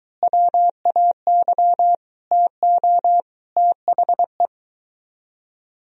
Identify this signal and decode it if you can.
Morse code, heard as a keyed tone, about 21 wpm, 700 Hz. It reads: WAY TO THE